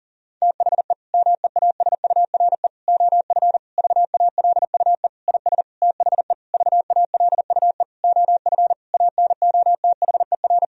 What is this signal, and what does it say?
Morse code, heard as a keyed tone, about 37 wpm, 700 Hz. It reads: THE MEASURE OF VALUE IS THE VALUE OF ANOTHER
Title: Sentences from Top 1000 Words - Review Part 3 - 40wpm